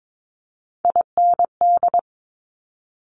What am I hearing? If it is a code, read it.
Morse code, heard as a keyed tone, about 22 wpm, 700 Hz. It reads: IND
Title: Multi-letter Phonemes Quick Fire 22wpm